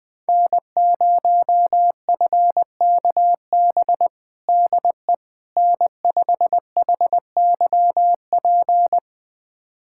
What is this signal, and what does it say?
Morse code, heard as a keyed tone, about 20 wpm, 700 Hz. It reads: N0FKB DE N5HYP
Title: USA Callsign DE Callsign - 5 Characters - 20wpm